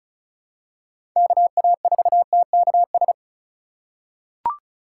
Morse code, heard as a keyed tone, about 36 wpm, 700 Hz. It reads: KA4TKS
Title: USA Call Signs - 6 Characters - 4x Character Spacing - 35wpm